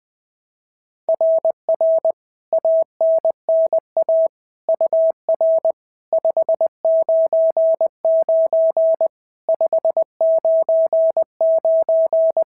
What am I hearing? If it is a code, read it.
Morse code, heard as a keyed tone, about 20 wpm, 690 Hz. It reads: RR ANNA UR 599 599
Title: SOTA QSO Practice 20wpm